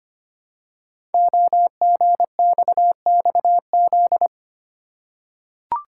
Morse code, heard as a keyed tone, about 25 wpm, 700 Hz. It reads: OGXXZ
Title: Code Groups - 5 Letter:Numbers - 25wpm